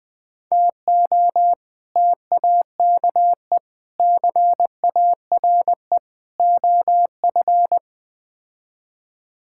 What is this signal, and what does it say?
Morse code, heard as a keyed tone, about 20 wpm, 700 Hz. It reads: TO TAKE CARE OF
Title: Build A Phrase - 4-Word Phrases - 20wpm